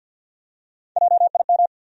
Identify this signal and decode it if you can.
Morse code, heard as a keyed tone, about 49 wpm, 700 Hz. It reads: JIM
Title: USA Common Names - Max 4 Letters - 50wpm